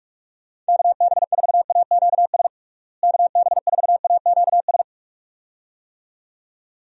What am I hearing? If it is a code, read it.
Morse code, heard as a keyed tone, about 46 wpm, 700 Hz. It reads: KB4AQS KB4AQS
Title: USA Call Signs - 6 Characters - Repeated 2x - 45wpm